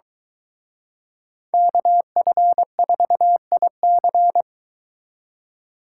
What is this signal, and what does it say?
Morse code, heard as a keyed tone, about 23 wpm, 700 Hz. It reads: KF4IC